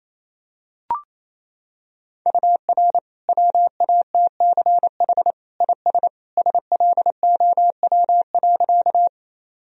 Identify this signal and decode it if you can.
Morse code, heard as a keyed tone, about 28 wpm, 700 Hz. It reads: UR WATCH IS SLOW.
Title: Sentences from Top 400 Words and Abbreviations 28wpm